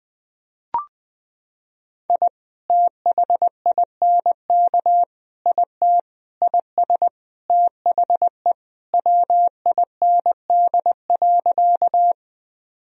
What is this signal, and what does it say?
Morse code, heard as a keyed tone, about 20 wpm, 700 Hz. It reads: I THINK IT IS THE WIND.
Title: Sentences from Top 1000 Words - Review Part 2 - 20wpm